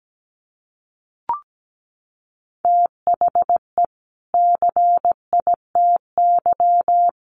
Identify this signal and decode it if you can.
Morse code, heard as a keyed tone, about 17 wpm, 700 Hz. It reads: THE CITY
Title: Sets of 2 Words 17wpm